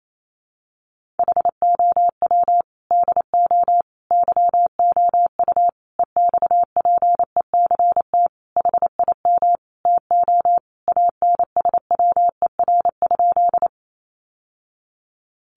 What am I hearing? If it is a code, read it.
Morse code, heard as a keyed tone, about 28 wpm, 700 Hz. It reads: HOW DO YOU EXPECT HIM TO ANSWER?
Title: Sentences from Top 1100 Words 28wpm